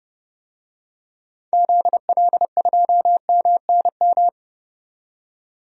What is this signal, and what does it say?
Morse code, heard as a keyed tone, about 32 wpm, 700 Hz. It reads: ZL2MNM
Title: DXCC-100 Call Signs - 6 Characters - 30wpm